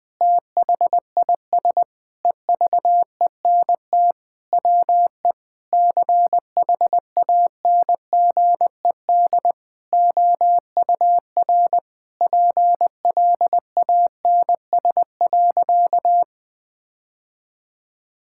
Morse code, heard as a keyed tone, about 20 wpm, 700 Hz. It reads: THIS EVENT WE CHANGED OUR PLANS.
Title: Sentences with 5-Word Phrases - 20wpm